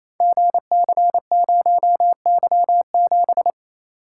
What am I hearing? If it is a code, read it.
Morse code, heard as a keyed tone, about 27 wpm, 700 Hz. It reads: GC0Y7
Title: Code Groups - 5 Letter:Numbers - 28wpm